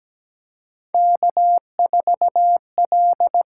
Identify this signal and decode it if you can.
Morse code, heard as a keyed tone, about 17 wpm, 700 Hz. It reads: K4L
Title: USA Call Signs - 6x Character Spacing - 17wpm